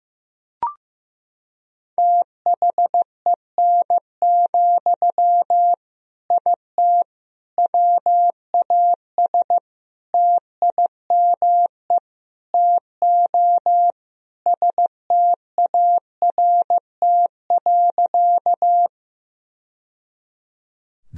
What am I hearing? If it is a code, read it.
Morse code, heard as a keyed tone, about 15 wpm, 700 Hz. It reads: THEN, IT WAS TIME TO START.
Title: Sentences from Top 1000 Words - Review Part 1 - 15wpm